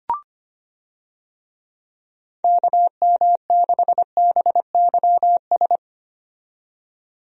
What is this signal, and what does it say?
Morse code, heard as a keyed tone, about 25 wpm, 700 Hz. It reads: KM6BYS
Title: USA Call Signs - 6 Characters - 25wpm